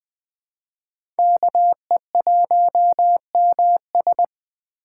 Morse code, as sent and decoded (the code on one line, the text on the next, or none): -.- . .---- -- ...
KE1MS